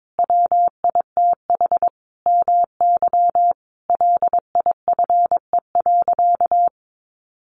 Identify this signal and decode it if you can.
Morse code, heard as a keyed tone, about 22 wpm, 700 Hz. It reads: WITH MY LIFE.
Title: Sentences from Top 1500 Words 22wpm